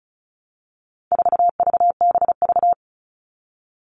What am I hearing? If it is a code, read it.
Morse code, heard as a keyed tone, about 34 wpm, 700 Hz. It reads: 4VBV
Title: All Cut Numbers - 4-Digit 35wpm